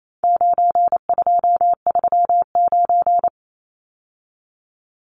Morse code, heard as a keyed tone, about 28 wpm, 700 Hz. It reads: 9239
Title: Numbers - 4-Digit 28wpm